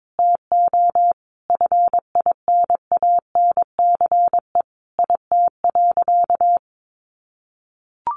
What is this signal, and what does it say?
Morse code, heard as a keyed tone, about 22 wpm, 700 Hz. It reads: TO FINANCE IT.
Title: Sentences from Top 1500 Words 22wpm